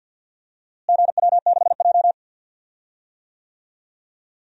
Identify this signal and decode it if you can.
Morse code, heard as a keyed tone, about 46 wpm, 700 Hz. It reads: GW6J
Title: DXCC-100 Call Signs - 4 Characters - Repeated 3x - 50wpm